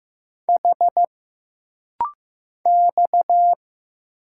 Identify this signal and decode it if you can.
Morse code, heard as a keyed tone, about 15 wpm, 700 Hz. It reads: H X
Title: Single Letters - Rapid-Fire 15wpm